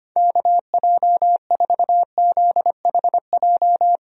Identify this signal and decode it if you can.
Morse code, heard as a keyed tone, about 25 wpm, 700 Hz. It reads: KJ4ZHJ
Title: USA Call Signs - 6 Characters - 25wpm